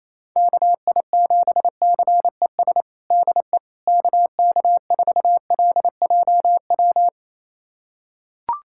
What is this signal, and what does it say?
Morse code, heard as a keyed tone, about 28 wpm, 700 Hz. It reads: KI7CES DE KK4LJW
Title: USA Callsign DE Callsign 28wpm